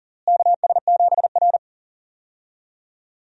Morse code, heard as a keyed tone, about 40 wpm, 700 Hz. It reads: KS7R
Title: USA Call Signs - 4 Characters - Repeated 3x - 40wpm